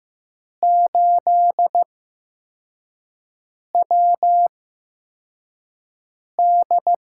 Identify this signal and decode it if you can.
Morse code, heard as a keyed tone, about 15 wpm, spaced as for 4 wpm, 700 Hz. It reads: 8WD
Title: USA Call Signs - 5 Characters - 8x Character Spacing - 15wpm